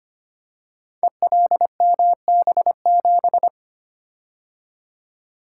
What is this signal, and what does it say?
Morse code, heard as a keyed tone, about 25 wpm, 700 Hz. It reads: ELMB7
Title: Code Groups - 5 Letter:Numbers - 25wpm